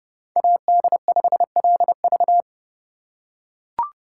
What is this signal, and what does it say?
Morse code, heard as a keyed tone, about 29 wpm, 700 Hz. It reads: AD5LV